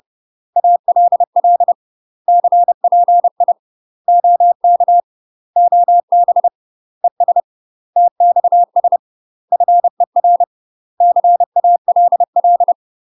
Morse code, heard as a keyed tone, about 30 wpm, 700 Hz. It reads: ALL CPI OK OB ES TXS FER CALL
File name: UK MASTER QSOs TRAINING / 30WPM / UK MASTER QSOs 0005